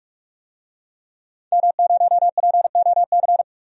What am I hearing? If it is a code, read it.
Morse code, heard as a keyed tone, about 46 wpm, 700 Hz. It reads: M0POC
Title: DXCC-100 Call Signs - 5 Characters - 45wpm